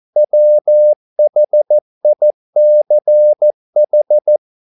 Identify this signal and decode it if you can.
Morse code, heard as a keyed tone, about 14 wpm, 600 Hz. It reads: WHICH
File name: mp3 files / a dogs mistake - 14wpm